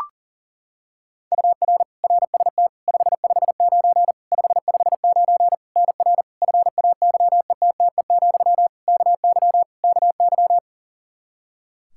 Morse code, heard as a keyed tone, about 37 wpm, 700 Hz. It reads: UR RST 559 559 NR FAYETTE, KY KY
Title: Practice QSO - Exchange 1 - Beginning - 40wpm